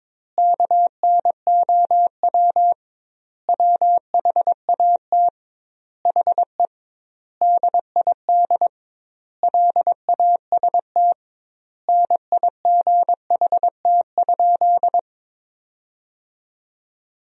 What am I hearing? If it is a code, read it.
Morse code, heard as a keyed tone, about 22 wpm, 700 Hz. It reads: KNOW WHAT HE DID LAST NIGHT?
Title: 2x Word-Spacing - Sentences from Top 300 Words - 22wpm